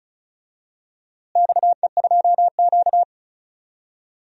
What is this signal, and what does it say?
Morse code, heard as a keyed tone, about 34 wpm, 700 Hz. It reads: XE2Q
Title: DXCC-100 Call Signs - 4 Characters - Repeated 3x - 35wpm